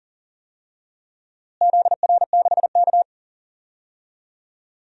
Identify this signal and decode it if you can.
Morse code, heard as a keyed tone, about 40 wpm, 700 Hz. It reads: ZR6K
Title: DXCC-100 Call Signs - 4 Characters - Repeated 3x - 40wpm